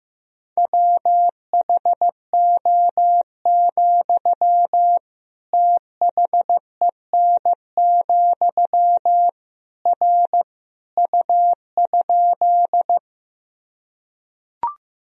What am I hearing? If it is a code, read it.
Morse code, heard as a keyed tone, about 16 wpm, 700 Hz. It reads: WHO, THEN, R U?
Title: Sentences from Top 100 Words and Abbreviations 15wpm